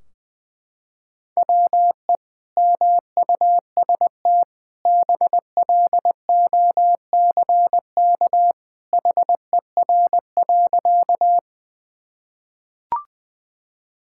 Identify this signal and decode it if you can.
Morse code, heard as a keyed tone, about 20 wpm, 700 Hz. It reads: WE MUST BLOCK HER.
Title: Sentences from Top 1600 Words 20wpm